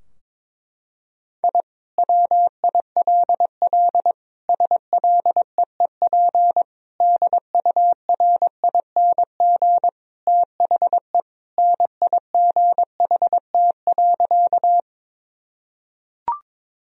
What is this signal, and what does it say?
Morse code, heard as a keyed tone, about 21 wpm, 700 Hz. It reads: I WILL SLEEP DURING THE NIGHT.
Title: Sentences from Top 1000 Words - Review Part 2 - 22wpm